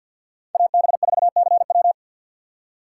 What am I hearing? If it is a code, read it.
Morse code, heard as a keyed tone, about 52 wpm, 700 Hz. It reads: AB4CW
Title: USA Call Signs - 5 Characters - 2x Character Spacing - 50wpm